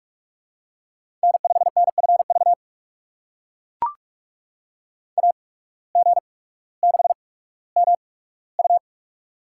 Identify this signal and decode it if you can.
Morse code, heard as a keyed tone, about 43 wpm, spaced as for 19 wpm, 700 Hz. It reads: N5NFV AG6MU
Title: USA Call Signs - 5 Characters - 8x Character Spacing - 45wpm